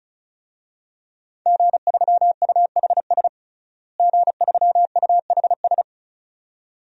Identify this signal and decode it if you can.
Morse code, heard as a keyed tone, about 36 wpm, 700 Hz. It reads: G3UHS G3UHS
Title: DXCC-100 Call Signs - 5 Characters - Repeated 2x - 35wpm